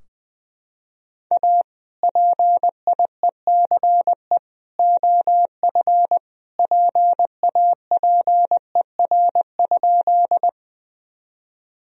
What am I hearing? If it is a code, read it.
Morse code, heard as a keyed tone, about 20 wpm, 700 Hz. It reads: A PIECE OF PAPER?
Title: Sentences from Top 1000 Words - Review Part 2 - 20wpm